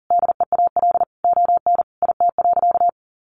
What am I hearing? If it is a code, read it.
Morse code, heard as a keyed tone, about 40 wpm, 700 Hz. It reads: DEAL ON IT.
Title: Sentences from Top 1300 Words 40wpm